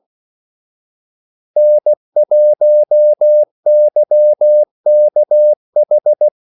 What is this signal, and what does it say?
Morse code, heard as a keyed tone, about 16 wpm, 600 Hz. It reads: N1YKH
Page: Kaldesignaler 16 wpm